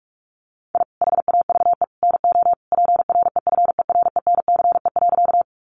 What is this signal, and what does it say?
Morse code, heard as a keyed tone, about 44 wpm, 700 Hz. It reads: I HAVE NO PREFERENCE.
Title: Sentences from Top 1500 Words 45wpm